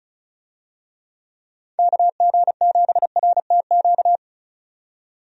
Morse code, heard as a keyed tone, about 34 wpm, 700 Hz. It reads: KG7RTQ
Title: USA Call Signs - 6 Characters - 35wpm